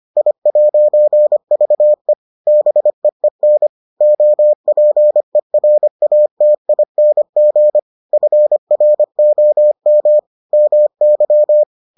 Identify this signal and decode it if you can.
Morse code, heard as a keyed tone, about 25 wpm, 600 Hz. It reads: I'VE BEEN OPERATING FROM MY